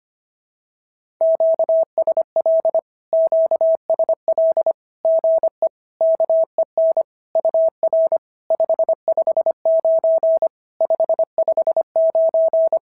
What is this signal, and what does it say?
Morse code, heard as a keyed tone, about 25 wpm, 650 Hz. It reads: QSL QSL GE KEN UR 559 559
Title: SOTA QSO Practice 25wpm